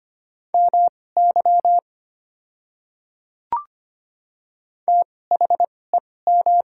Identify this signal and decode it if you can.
Morse code, heard as a keyed tone, about 25 wpm, spaced as for 18 wpm, 690 Hz. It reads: MY THEM
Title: 2x Character Spacing - Top 100 Words - 25wpm